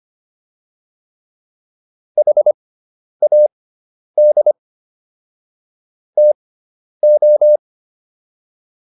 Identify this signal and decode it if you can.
Morse code, heard as a keyed tone, about 25 wpm, spaced as for 10 wpm, 600 Hz. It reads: HAD TO